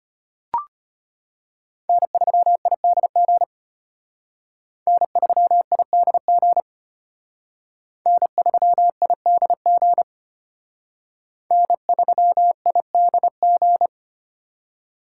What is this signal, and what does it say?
Morse code, heard as a keyed tone, about 31 wpm, 700 Hz. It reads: N3IDG N3IDG N3IDG N3IDG